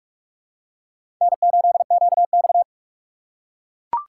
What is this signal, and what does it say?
Morse code, heard as a keyed tone, about 42 wpm, 700 Hz. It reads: N8QX
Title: USA Call Signs - 4 Characters - 6x Character Spacing - 45wpm